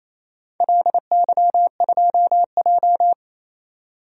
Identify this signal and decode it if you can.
Morse code, heard as a keyed tone, about 28 wpm, 700 Hz. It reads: LY2J
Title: DXCC-100 Call Signs - 4 Characters - 28wpm